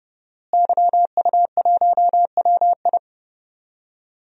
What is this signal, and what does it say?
Morse code, heard as a keyed tone, about 29 wpm, 700 Hz. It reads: YU1WI
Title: DXCC-100 Call Signs - 5 Characters - Repeated 2x - 30wpm